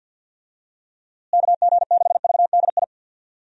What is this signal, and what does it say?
Morse code, heard as a keyed tone, about 52 wpm, 700 Hz. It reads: KG6VDI
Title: USA Call Signs - 6 Characters - Repeated 3x - 50wpm